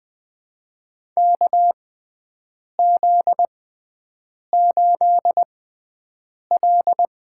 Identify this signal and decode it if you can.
Morse code, heard as a keyed tone, about 20 wpm, spaced as for 7 wpm, 700 Hz. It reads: KZ8L
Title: USA Call Signs - 4 Characters - 6x Character Spacing - 20wpm